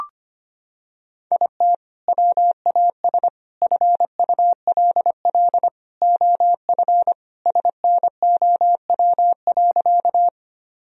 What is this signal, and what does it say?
Morse code, heard as a keyed tone, about 25 wpm, 700 Hz. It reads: IT WAS FULL OF SNOW.
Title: Sentences from Top 1000 Words - Review Part 3 - 25wpm